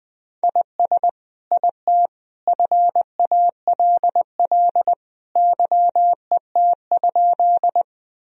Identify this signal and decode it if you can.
Morse code, heard as a keyed tone, about 21 wpm, 700 Hz. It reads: IS IT FALL YET?
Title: Sentences from Top 1000 Words - Review Part 2 - 20wpm